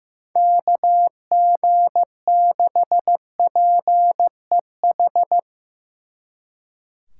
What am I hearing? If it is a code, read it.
Morse code, heard as a keyed tone, about 17 wpm, 700 Hz. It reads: KG6PEH
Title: Speed Racing - USA Call Signs 15wpm